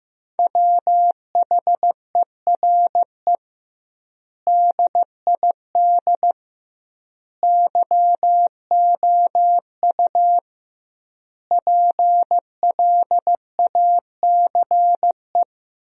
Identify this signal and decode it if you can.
Morse code, heard as a keyed tone, about 15 wpm, 700 Hz. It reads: WHERE DID YOU PLACE
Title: 2x Word-Spacing - Sentences from Top 200 Words - 15wpm